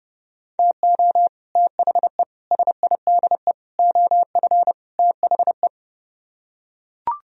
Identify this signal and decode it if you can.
Morse code, heard as a keyed tone, about 32 wpm, 700 Hz. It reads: TO THE SIDE OF THE